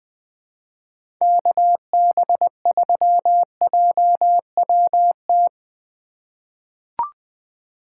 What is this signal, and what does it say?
Morse code, heard as a keyed tone, about 20 wpm, 700 Hz. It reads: KB3JWT
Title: USA Call Signs - 6 Characters - 4x Character Spacing - 20wpm